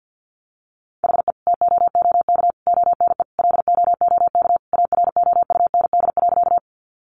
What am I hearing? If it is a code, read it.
Morse code, heard as a keyed tone, about 50 wpm, 700 Hz. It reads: HE TOOK ONE LOOK AROUND.
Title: Sentences from Top 600 Words 50wpm